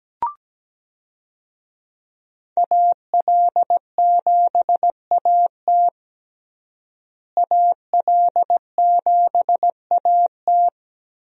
Morse code, heard as a keyed tone, about 17 wpm, 700 Hz. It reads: AL7AT AL7AT
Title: USA Call Signs - 5 Characters - Repeated 3x - 17wpm